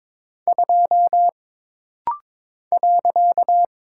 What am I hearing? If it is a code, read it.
Morse code, heard as a keyed tone, about 22 wpm, 700 Hz. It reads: 2 .
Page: This rapid-fire timing is geared to push your ICR (Instant Character Recognition) skills.